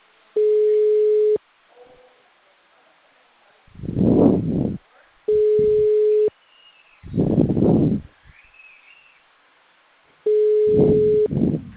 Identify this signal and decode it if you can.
Ringback tone (Europe, 425 Hz).
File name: out- -2009- - - .15